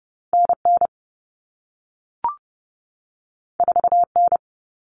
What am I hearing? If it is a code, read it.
Morse code, heard as a keyed tone, about 30 wpm, 700 Hz. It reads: NN 4N
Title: All Cut Numbers - 2-Digit 30wpm